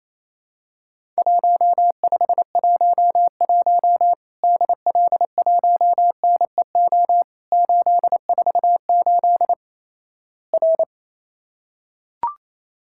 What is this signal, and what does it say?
Morse code, heard as a keyed tone, about 28 wpm, 700 Hz. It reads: 1511 DL1NEO 848 R
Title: Worked All Europe DX Contest - QTC Exchange - 28wpm